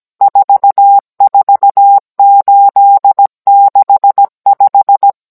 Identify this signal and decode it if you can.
Morse code, heard as a keyed tone, about 17 wpm, 800 Hz. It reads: 44865